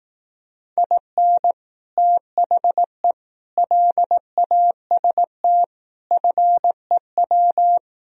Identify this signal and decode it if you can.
Morse code, heard as a keyed tone, about 18 wpm, 700 Hz. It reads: IN THE LAST FEW
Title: Speed-Racing - Sets of 5 Words 15wpm